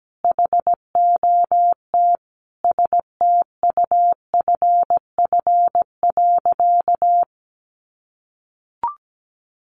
Morse code, heard as a keyed tone, about 17 wpm, 700 Hz. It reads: HOT STUFF.
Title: Sentences from Top 1400 Words 17wpm